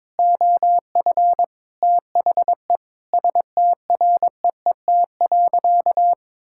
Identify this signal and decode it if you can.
Morse code, heard as a keyed tone, about 22 wpm, 700 Hz. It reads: OF THE STREET.
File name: practice / 22wpm / Sentences with 2-Word Phrases - 22wpm